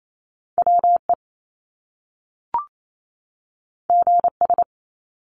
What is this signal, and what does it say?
Morse code, heard as a keyed tone, about 28 wpm, 700 Hz. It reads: WE GS
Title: Top Two Letter Combinations in Words 28wpm